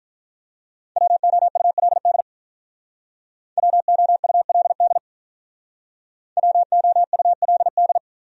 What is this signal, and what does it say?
Morse code, heard as a keyed tone, about 45 wpm, 700 Hz. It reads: WOULD WOULD WOULD